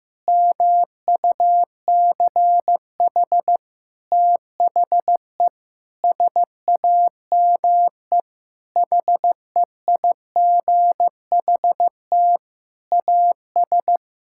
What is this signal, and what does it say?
Morse code, heard as a keyed tone, about 15 wpm, 700 Hz. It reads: MUCH THE SAME HEIGHT AS
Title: Sentences from Top 1600 Words 15wpm